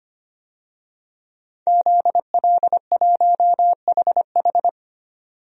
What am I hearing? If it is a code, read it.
Morse code, heard as a keyed tone, about 25 wpm, 700 Hz. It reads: ZL1HH
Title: DXCC-100 Call Signs - 5 Characters - 25wpm